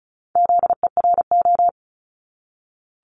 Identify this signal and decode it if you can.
Morse code, heard as a keyed tone, about 36 wpm, 700 Hz. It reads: ZERO
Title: Multi-letter Phonemes 35wpm